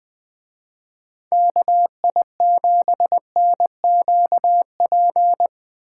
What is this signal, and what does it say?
Morse code, heard as a keyed tone, about 20 wpm, 700 Hz. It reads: KI7NQP